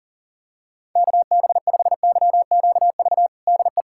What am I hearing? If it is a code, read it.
Morse code, heard as a keyed tone, about 40 wpm, 700 Hz. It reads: KB5YQV DE
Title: USA Callsign DE Callsign 40wpm